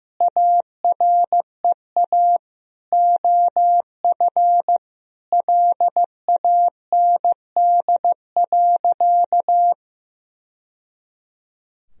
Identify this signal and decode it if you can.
Morse code, heard as a keyed tone, about 15 wpm, 700 Hz. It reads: AREA OF LAND.
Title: Sentences from Top 1000 Words - Review Part 2 - 15wpm